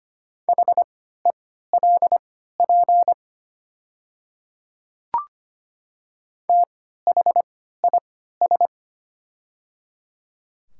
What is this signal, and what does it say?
Morse code, heard as a keyed tone, about 25 wpm, spaced as for 14 wpm, 700 Hz. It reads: HELP THIS
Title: 3x Character Spacing - Top 300 Words - 25wpm